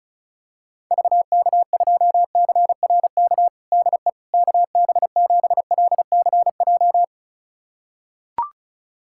Morse code, heard as a keyed tone, about 36 wpm, 700 Hz. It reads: VK2CRK DE KB7LCJ
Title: USA Callsign DE Callsign 35wpm